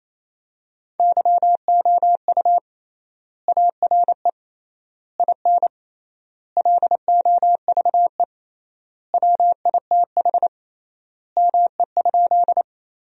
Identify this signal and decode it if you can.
Morse code, heard as a keyed tone, about 28 wpm, 700 Hz. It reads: YOU ARE IN LOVE WITH ME?
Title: 3x Word-Spacing - Sentences from Top 400 Words - 28wpm